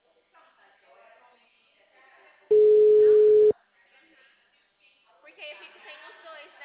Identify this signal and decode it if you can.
Ringback tone (Europe, 425 Hz), with speech.